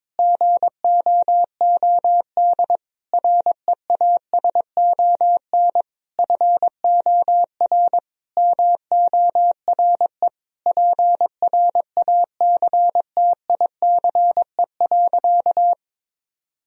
Morse code, heard as a keyed tone, about 22 wpm, 700 Hz. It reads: GOOD REASON FOR MORE PRACTICE.
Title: Sentences from Top 1000 Words - Review Part 4 - 22wpm